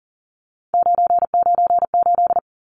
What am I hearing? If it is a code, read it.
Morse code, heard as a keyed tone, about 40 wpm, 700 Hz. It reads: 998
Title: Numbers - 3-Digit 40wpm